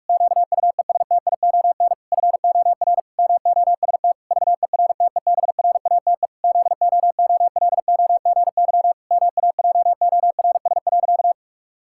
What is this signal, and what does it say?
Morse code, heard as a keyed tone, about 46 wpm, 700 Hz. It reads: QUESTION FOR MOST VERTEBRATE ZOOLOGY MAJORS.
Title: Pangram Challenge v2 - 45wpm